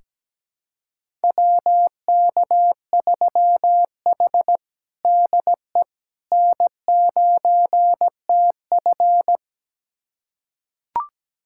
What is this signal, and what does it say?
Morse code, heard as a keyed tone, about 17 wpm, 700 Hz. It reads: WK3H DE N9TF
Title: USA Callsign DE Callsign - 4 Characters - 17wpm